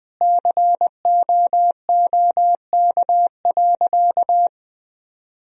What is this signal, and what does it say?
Morse code, heard as a keyed tone, about 20 wpm, 700 Hz. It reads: COOK.
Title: Sentences from Top 1000 Words - Review Part 4 - 20wpm